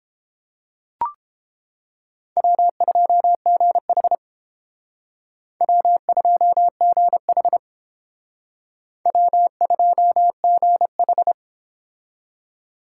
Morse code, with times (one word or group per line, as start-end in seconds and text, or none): 2.4-4.2 W2GH
5.6-7.6 W2GH
9.1-11.3 W2GH